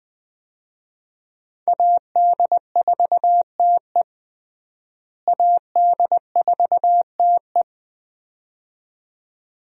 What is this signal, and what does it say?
Morse code, heard as a keyed tone, about 20 wpm, 700 Hz. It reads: AD4TE AD4TE
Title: USA Call Signs - 5 Characters - Repeated 2x - 20wpm